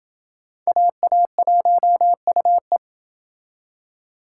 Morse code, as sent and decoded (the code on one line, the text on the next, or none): .- .- .---- ..- .
AA1UE